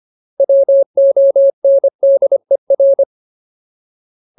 Morse code, heard as a keyed tone, about 25 wpm, 550 Hz. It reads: WONDER
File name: Ultimate Word List / 25WPM Ultimate